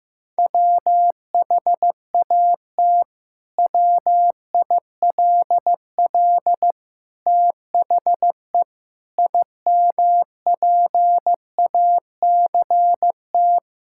Morse code, heard as a keyed tone, about 15 wpm, 700 Hz. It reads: WHAT WILL THE IMPACT
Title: Sentences from Top 1000 Words - Review Part 6 - 15wpm